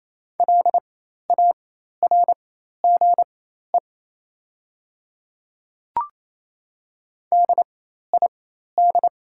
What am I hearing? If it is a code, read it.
Morse code, heard as a keyed tone, about 28 wpm, spaced as for 13 wpm, 690 Hz. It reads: LARGE DID
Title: 4x Character Spacing - Top 200 Words - 28wpm